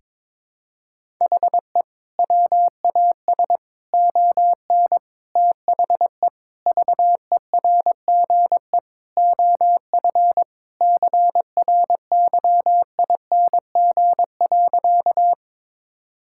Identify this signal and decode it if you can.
Morse code, heard as a keyed tone, about 22 wpm, 700 Hz. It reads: HE WAS ON THE VERGE OF CRYING.
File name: practice / 22wpm / Sentences with 4-Word Phrases - 22wpm